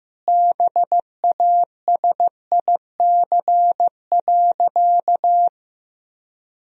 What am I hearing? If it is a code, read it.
Morse code, heard as a keyed tone, about 15 wpm, 700 Hz. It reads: BASIC.
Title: Sentences from Top 1000 Words - Review Part 3 - 15wpm